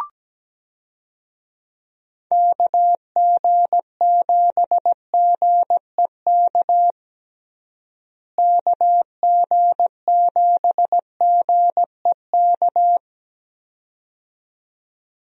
Morse code, heard as a keyed tone, about 17 wpm, 700 Hz. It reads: KG7GEK KG7GEK
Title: USA Call Signs - 6 Characters - Repeated 2x - 17wpm